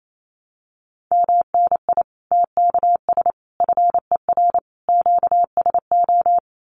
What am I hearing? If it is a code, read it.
Morse code, heard as a keyed tone, about 28 wpm, 700 Hz. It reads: MNI TKS FER QSO
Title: Common QSO Phrases 28wpm